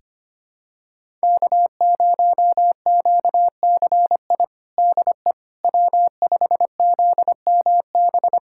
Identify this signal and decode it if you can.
Morse code, heard as a keyed tone, about 25 wpm, 700 Hz. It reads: K0QCI DE W5ZMB
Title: USA Callsign DE Callsign - 5 Characters - 25wpm